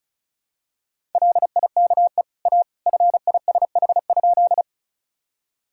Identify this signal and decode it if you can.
Morse code, heard as a keyed tone, about 36 wpm, 700 Hz. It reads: LIKE A FISH?
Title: Sentences from Top 1000 Words - Review Part 1 - 35wpm